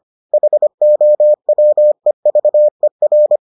Morse code, heard as a keyed tone, about 25 wpm, 600 Hz. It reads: HOWEVER